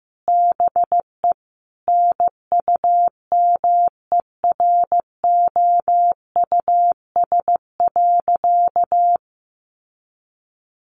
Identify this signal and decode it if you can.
Morse code, heard as a keyed tone, about 15 wpm, 700 Hz. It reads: BE NUMEROUS.
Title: Sentences from Top 1400 Words 15wpm